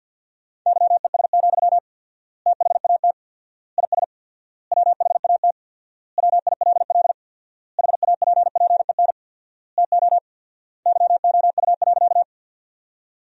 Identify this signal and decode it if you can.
Morse code, heard as a keyed tone, about 48 wpm, 700 Hz. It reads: YES, THAT IS WHAT WILL HAPPEN TO YOU.
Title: 4x Word-Spacing - Sentences from Top 500 Words - 50wpm